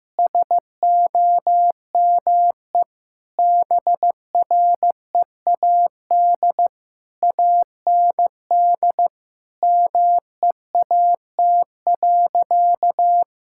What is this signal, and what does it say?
Morse code, heard as a keyed tone, about 15 wpm, 700 Hz. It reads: SOME BREAD AND MEAT.
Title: Sentences from Top 1600 Words 15wpm